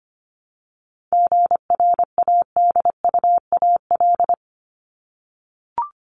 Morse code, heard as a keyed tone, about 25 wpm, 700 Hz. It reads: GRADUAL
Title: LY Suffix - Top 100 Words with Base Words 25wpm